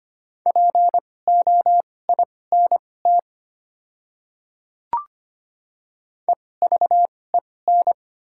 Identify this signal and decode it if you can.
Morse code, heard as a keyed tone, about 25 wpm, spaced as for 18 wpm, 700 Hz. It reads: POINT EVEN
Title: 2x Character Spacing - Top 200 Words - 25wpm